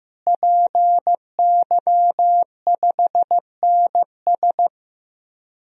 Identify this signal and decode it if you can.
Morse code, heard as a keyed tone, about 15 wpm, 700 Hz. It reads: PY5NS
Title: DXCC-100 Call Signs - 5 Characters - Repeated 3x - 15wpm